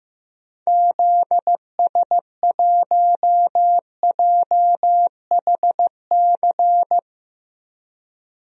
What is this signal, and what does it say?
Morse code, heard as a keyed tone, about 15 wpm, 700 Hz. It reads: ZS1JHC
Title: DXCC-100 Call Signs - 6 Characters - Repeated 3x - 15wpm